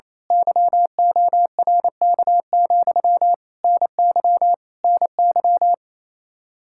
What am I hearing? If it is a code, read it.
Morse code, heard as a keyed tone, about 28 wpm, 700 Hz. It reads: YORK, NY NY
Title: Speed-Racing Common QSO Phrases Encore - 28wpm